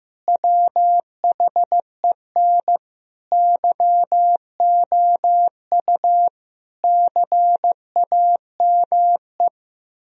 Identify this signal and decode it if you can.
Morse code, heard as a keyed tone, about 15 wpm, 700 Hz. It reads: WHEN YOU CAME
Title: Sentences from Top 1000 Words - Review Part 3 - 15wpm